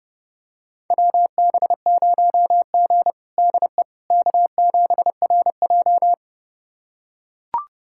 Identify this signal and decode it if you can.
Morse code, heard as a keyed tone, about 30 wpm, 700 Hz. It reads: WB0G DE K7RJ
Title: USA Callsign DE Callsign - 4 Characters - 30wpm